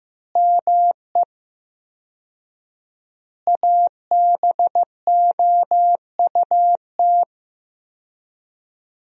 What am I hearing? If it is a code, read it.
Morse code, heard as a keyed tone, about 15 wpm, 700 Hz. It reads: ME ABOUT
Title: 4x Word-Spacing - Sentences from Top 200 Words - 15wpm